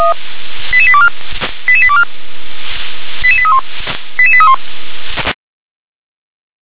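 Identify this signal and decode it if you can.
Five-tone selective call (ZVEI), heard as quick runs of five tones, tones of about 70 ms. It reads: 88824 88824 88831 88831
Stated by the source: ZVEI five-tone call 88832 received by SDR.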